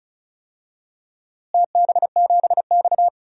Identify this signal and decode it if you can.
Morse code, heard as a keyed tone, about 36 wpm, 700 Hz. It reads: TB7X
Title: DXCC-100 Call Signs - 4 Characters - Repeated 2x - 35wpm